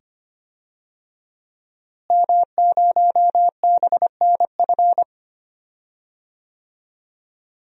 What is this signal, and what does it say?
Morse code, heard as a keyed tone, about 25 wpm, 700 Hz. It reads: M0BNF
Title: DXCC-100 Call Signs - 5 Characters - Repeated 2x - 25wpm